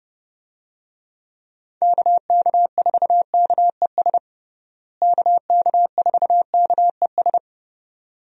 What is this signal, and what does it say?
Morse code, heard as a keyed tone, about 30 wpm, 700 Hz. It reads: KK4KES KK4KES
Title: USA Call Signs - 6 Characters - Repeated 2x - 30wpm